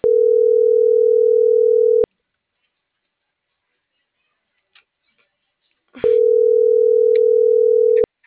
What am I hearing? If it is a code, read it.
Ringback tone (North America), 440 + 480 Hz.